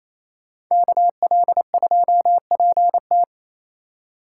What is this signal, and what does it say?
Morse code, heard as a keyed tone, about 28 wpm, 700 Hz. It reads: KL2PT
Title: USA Call Signs - 4x Character Spacing - 28wpm